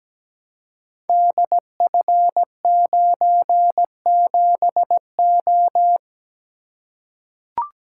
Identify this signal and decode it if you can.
Morse code, heard as a keyed tone, about 17 wpm, 700 Hz. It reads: DF97O
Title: Code Groups - 5 Letter:Numbers - 17wpm